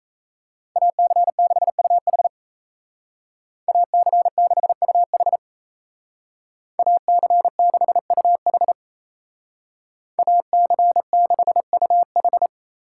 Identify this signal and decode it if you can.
Morse code, heard as a keyed tone, about 36 wpm, 700 Hz. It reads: AC6UH AC6UH AC6UH AC6UH